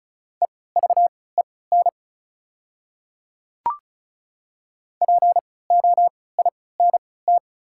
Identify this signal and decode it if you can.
Morse code, heard as a keyed tone, about 34 wpm, spaced as for 20 wpm, 700 Hz. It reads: EVEN POINT
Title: 3x Character Spacing - Top 300 Words - 35wpm